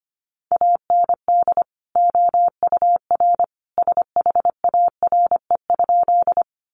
Morse code, heard as a keyed tone, about 25 wpm, 700 Hz. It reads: AND OUR SHARE?
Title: Sentences from Top 900 Words 25wpm